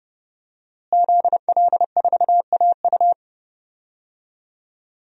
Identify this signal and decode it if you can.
Morse code, heard as a keyed tone, about 32 wpm, 700 Hz. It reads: ZL4AU
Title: DXCC-100 Call Signs - 5 Characters - Repeated 2x - 30wpm